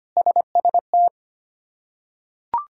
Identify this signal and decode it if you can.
Morse code, heard as a keyed tone, about 25 wpm, 700 Hz. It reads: SST
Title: K1USN SST QSO Practice - 25wpm